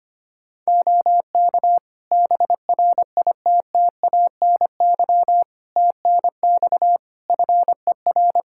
Morse code, heard as a keyed tone, about 25 wpm, 700 Hz. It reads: OK BRITTANY TNX FER
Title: K1USN SST QSO Practice 25wpm